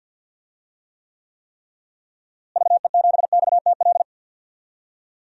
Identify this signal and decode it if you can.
Morse code, heard as a keyed tone, about 51 wpm, 700 Hz. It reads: VE7XTL
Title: DXCC-100 Call Signs - 6 Characters - Repeated 3x - 50wpm